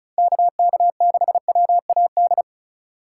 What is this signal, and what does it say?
Morse code, heard as a keyed tone, about 36 wpm, 700 Hz. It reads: KK6WAD
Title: USA Call Signs - 6 Characters - 3x Character Spacing - 35wpm